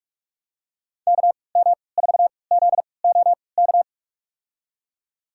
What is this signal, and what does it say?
Morse code, heard as a keyed tone, about 43 wpm, spaced as for 25 wpm, 700 Hz. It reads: KM4ZOK
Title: USA Call Signs - 6 Characters - 3x Character Spacing - 45wpm